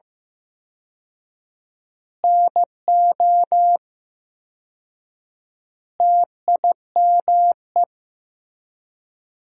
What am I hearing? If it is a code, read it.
Morse code, heard as a keyed tone, about 15 wpm, 700 Hz. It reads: NO TIME
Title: 4x Word-Spacing - Sentences from Top 400 Words - 15wpm